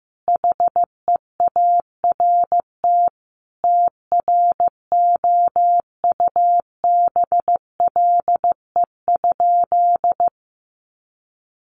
Morse code, heard as a keyed tone, about 15 wpm, 700 Hz. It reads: HEART TROUBLE?
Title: Sentences from Top 1200 Words 15wpm